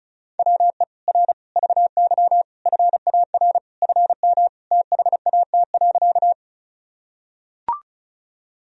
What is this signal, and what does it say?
Morse code, heard as a keyed tone, about 36 wpm, 700 Hz. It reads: WE R VY FAR FM THAT.
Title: Sentences from Top 300 Words and Abbreviations 35wpm